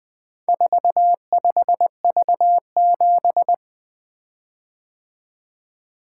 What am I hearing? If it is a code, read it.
Morse code, heard as a keyed tone, about 20 wpm, 700 Hz. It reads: 45V7
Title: Four Letter-Number Combinations - 20wpm